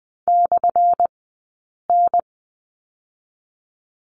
Morse code, heard as a keyed tone, about 20 wpm, spaced as for 8 wpm, 700 Hz. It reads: /N
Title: 177 Lesson 29d - Slash - Two Characters 20wpm